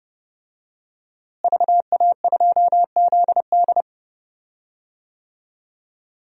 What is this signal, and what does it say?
Morse code, heard as a keyed tone, about 30 wpm, 700 Hz. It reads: VA2ZD
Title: DXCC-100 Call Signs - 5 Characters - Repeated 3x - 30wpm